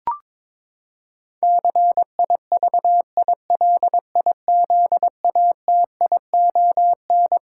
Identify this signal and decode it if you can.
Morse code, heard as a keyed tone, about 22 wpm, 700 Hz. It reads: CIVILIZATION
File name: Twelve Letter Words - 22wpm